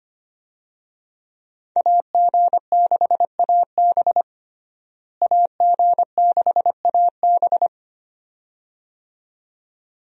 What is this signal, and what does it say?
Morse code, heard as a keyed tone, about 25 wpm, 700 Hz. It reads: AG6AB AG6AB
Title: USA Call Signs - 5 Characters - Repeated 2x - 25wpm